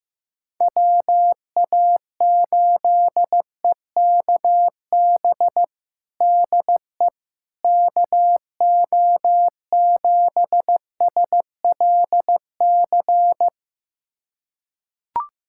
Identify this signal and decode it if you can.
Morse code, heard as a keyed tone, about 15 wpm, 700 Hz. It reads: WA8EKB DE KO7SLC
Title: USA Callsign DE Callsign - 6 Characters - 15wpm